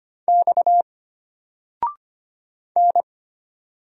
Morse code, heard as a keyed tone, about 25 wpm, 700 Hz. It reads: X N
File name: Single-Letter-Number-25wpm